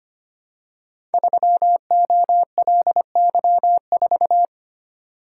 Morse code, heard as a keyed tone, about 25 wpm, 700 Hz. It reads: 3OLY4
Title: Code Groups - 5 Letter:Numbers - 25wpm